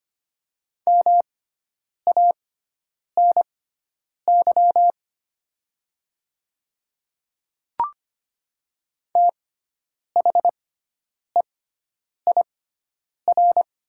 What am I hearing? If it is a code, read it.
Morse code, heard as a keyed tone, about 25 wpm, spaced as for 9 wpm, 700 Hz. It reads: MANY THEIR
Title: 6x Character Spacing - Top 100 Words - 25wpm